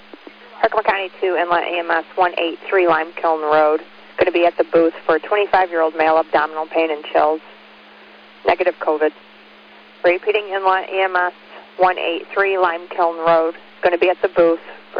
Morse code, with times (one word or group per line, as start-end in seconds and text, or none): none